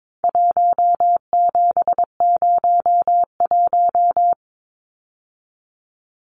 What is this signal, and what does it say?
Morse code, heard as a keyed tone, about 22 wpm, 700 Hz. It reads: 1701
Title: Numbers - 4-Digit 22wpm